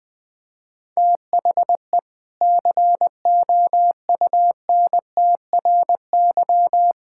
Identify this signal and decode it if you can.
Morse code, heard as a keyed tone, about 20 wpm, 700 Hz. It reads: THE COUNTRY
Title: Sentences from Top 1000 Words - Review Part 5 - 20wpm